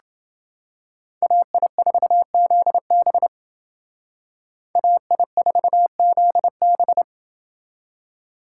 Morse code, heard as a keyed tone, about 28 wpm, 700 Hz. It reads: AI4ZB AI4ZB